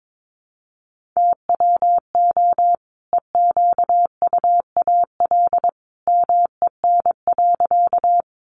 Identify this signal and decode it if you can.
Morse code, heard as a keyed tone, about 22 wpm, 700 Hz. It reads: TWO EQUAL MEN.
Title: Sentences from Top 1400 Words 22wpm